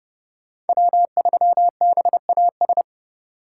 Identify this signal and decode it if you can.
Morse code, heard as a keyed tone, about 36 wpm, 700 Hz. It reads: W3BAS
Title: Speed Racing - USA 1x3 Call Signs 30wpm